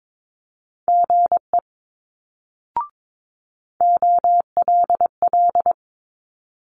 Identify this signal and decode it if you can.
Morse code, heard as a keyed tone, about 22 wpm, 700 Hz. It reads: GE OLL
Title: Multi-letter Phonemes Quick Fire 22wpm